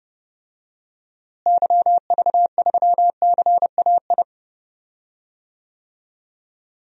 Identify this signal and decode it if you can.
Morse code, heard as a keyed tone, about 30 wpm, 700 Hz. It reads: YV3CAI
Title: DXCC-100 Call Signs - 6 Characters - Repeated 3x - 30wpm